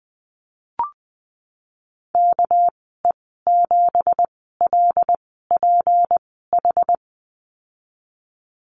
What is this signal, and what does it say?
Morse code, heard as a keyed tone, about 19 wpm, spaced as for 14 wpm, 700 Hz. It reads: KE7LPH
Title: 238 Lesson 38g - Letter q - Call Signs 20wpm